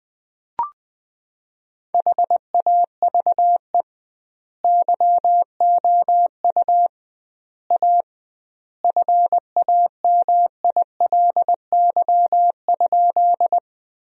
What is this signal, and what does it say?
Morse code, heard as a keyed tone, about 21 wpm, 700 Hz. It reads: HAVE YOU A FAMILY?
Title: 2x Word-Spacing - Sentences from Top 400 Words - 20wpm